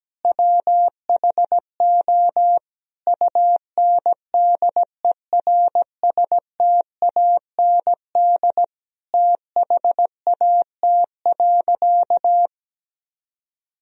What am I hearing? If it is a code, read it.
Morse code, heard as a keyed tone, about 17 wpm, 700 Hz. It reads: WHO UNDERSTAND THAT.
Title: Sentences from Top 1000 Words - Review Part 1 - 17wpm